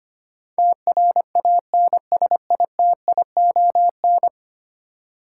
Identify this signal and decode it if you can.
Morse code, heard as a keyed tone, about 25 wpm, 690 Hz. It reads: TRANSITION
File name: 10 Letter Words - First 3 Letters Given - 25wpm